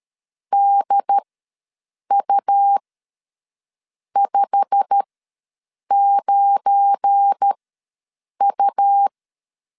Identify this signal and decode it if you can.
Morse code, heard as a keyed tone, about 13 wpm, spaced as for 6 wpm, 800 Hz. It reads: DU59U
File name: LEZIONI / 10 SIGNS / 10A SLASH ws 12wpm 3443